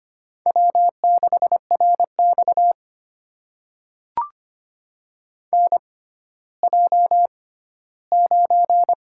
Morse code, heard as a keyed tone, about 25 wpm, spaced as for 14 wpm, 700 Hz. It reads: W6RX NJ9
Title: USA Call Signs - 4 Characters - 6x Character Spacing - 25wpm